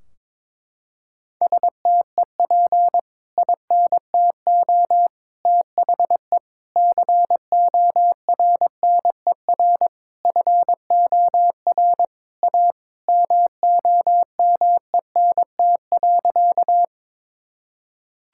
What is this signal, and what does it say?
Morse code, heard as a keyed tone, about 22 wpm, 700 Hz. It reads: STEP INTO THE CORNER FOR A MOMENT.
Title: Sentences from Top 1600 Words 22wpm